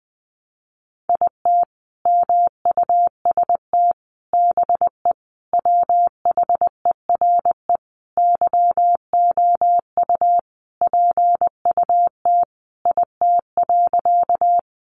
Morse code, heard as a keyed tone, about 20 wpm, 700 Hz. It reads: IT MUST BE WHERE YOU PUT IT.
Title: Sentences from Top 200 Words 20wpm